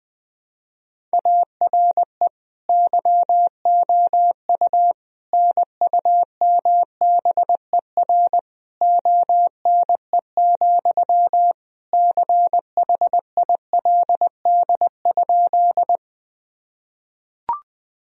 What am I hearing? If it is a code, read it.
Morse code, heard as a keyed tone, about 20 wpm, 700 Hz. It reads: ARE YOU NUMBER ONE, CHILD?
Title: Sentences from Top 1000 Words - Review Part 1 - 20wpm